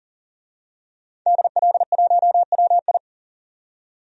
Morse code, heard as a keyed tone, about 38 wpm, 700 Hz. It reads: DL1WI
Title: DXCC-100 Call Signs - 5 Characters - 40wpm